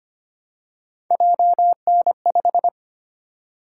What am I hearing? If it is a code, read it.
Morse code, heard as a keyed tone, about 30 wpm, 700 Hz. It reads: JN5
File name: Speed-Racing - Three Letter-Number Combinations - 25wpm